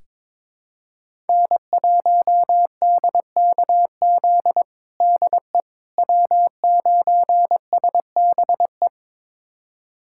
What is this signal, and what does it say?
Morse code, heard as a keyed tone, about 22 wpm, 700 Hz. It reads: N1DKZ DE W9SBE
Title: USA Callsign DE Callsign - 5 Characters - 22wpm